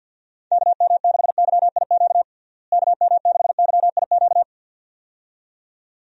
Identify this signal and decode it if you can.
Morse code, heard as a keyed tone, about 46 wpm, 700 Hz. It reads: KM6YIQ KM6YIQ
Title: USA Call Signs - 6 Characters - Repeated 2x - 50wpm